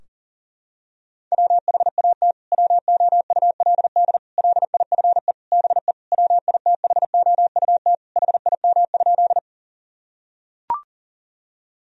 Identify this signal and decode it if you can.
Morse code, heard as a keyed tone, about 40 wpm, 700 Hz. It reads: WHAT WOULD LIFE BE WITHOUT HIM?
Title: Sentences from Top 1000 Words - Review Part 3 - 40wpm